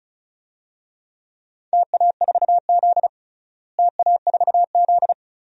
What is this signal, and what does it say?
Morse code, heard as a keyed tone, about 35 wpm, 700 Hz. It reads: TA4Z TA4Z
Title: DXCC-100 Call Signs - 4 Characters - Repeated 2x - 35wpm